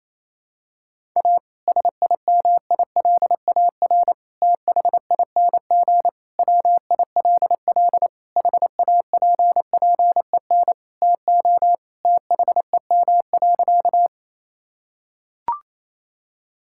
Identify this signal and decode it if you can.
Morse code, heard as a keyed tone, about 28 wpm, 700 Hz. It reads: A SIMILAR THING WILL HAPPEN TO THEM.
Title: Sentences from Top 1000 Words - Review Part 5 - 28wpm